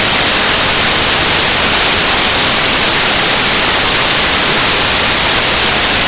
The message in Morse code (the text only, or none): none